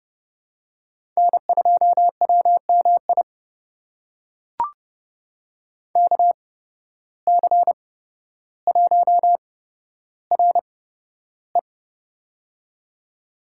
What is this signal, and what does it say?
Morse code, heard as a keyed tone, about 30 wpm, spaced as for 13 wpm, 700 Hz. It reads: N2WMI KC1RE
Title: USA Call Signs - 5 Characters - 8x Character Spacing - 30wpm